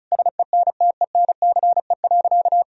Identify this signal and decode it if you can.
Morse code, heard as a keyed tone, about 34 wpm, 700 Hz. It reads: SENTENCE.
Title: Sentences from Top 1000 Words - Review Part 1 - 35wpm